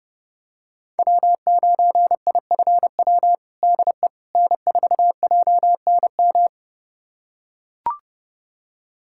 Morse code, heard as a keyed tone, about 30 wpm, 700 Hz. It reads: W9IFW DE N4JNM
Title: USA Callsign DE Callsign - 5 Characters - 30wpm